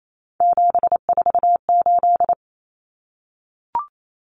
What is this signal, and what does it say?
Morse code, heard as a keyed tone, about 28 wpm, 700 Hz. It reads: 748
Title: Common Cut Numbers - 3-Digit 28wpm